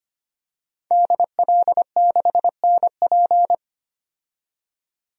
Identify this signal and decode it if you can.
Morse code, heard as a keyed tone, about 25 wpm, 700 Hz. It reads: DL6NP
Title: DXCC-100 Call Signs - 5 Characters - 25wpm